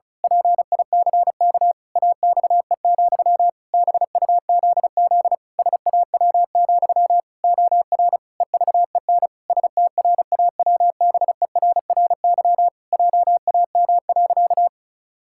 Morse code, heard as a keyed tone, about 36 wpm, 700 Hz. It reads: PICK AXE, BUZZ SAW, OR EVEN STRAWBERRY JAM.
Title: Pangram Challenge v2 - 35wpm